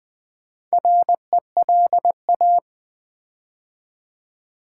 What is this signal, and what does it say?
Morse code, heard as a keyed tone, about 20 wpm, 700 Hz. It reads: RELA
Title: Build A Word - 4 Letters At A Time - 12-Letter Words - 20wpm